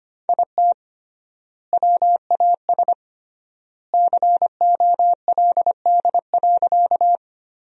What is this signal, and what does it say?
Morse code, heard as a keyed tone, about 25 wpm, 700 Hz. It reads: IT WAS COLD.
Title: 3x Word-Spacing - Sentences from Top 400 Words - 25wpm